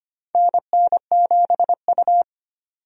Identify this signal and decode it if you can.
Morse code, heard as a keyed tone, about 25 wpm, 700 Hz. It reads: NN7U
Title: USA Call Signs - 4 Characters - 25wpm